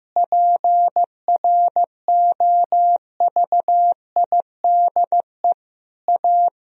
Morse code, heard as a keyed tone, about 15 wpm, 700 Hz. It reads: PROVIDE A
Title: Sentences from Top 1000 Words - Review Part 3 - 15wpm